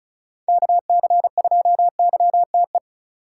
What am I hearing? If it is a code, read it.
Morse code, heard as a keyed tone, about 36 wpm, 700 Hz. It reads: KC2YTE
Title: USA Call Signs - 6 Characters - 3x Character Spacing - 35wpm